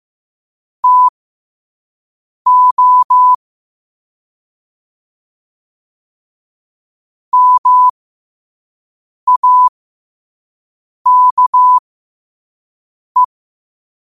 Morse code, heard as a keyed tone, about 15 wpm, spaced as for 5 wpm, 1000 Hz. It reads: TO MAKE